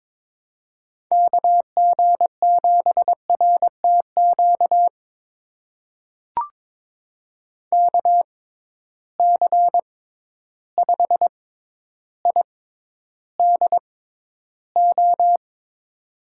Morse code, heard as a keyed tone, about 22 wpm, spaced as for 11 wpm, 700 Hz. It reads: KG7RTQ KC5IDO
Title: USA Call Signs - 6 Characters - 6x Character Spacing - 22wpm